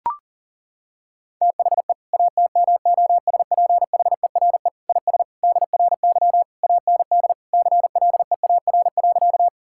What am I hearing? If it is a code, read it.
Morse code, heard as a keyed tone, about 37 wpm, 700 Hz. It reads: THE ATMOSPHERE IS DRY AND CLEAR.
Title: Sentences from Top 1600 Words 40wpm